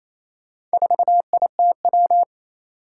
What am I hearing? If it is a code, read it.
Morse code, heard as a keyed tone, about 27 wpm, 700 Hz. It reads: 4ITW
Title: Four Challenging Letter-Number Combinations 28wpm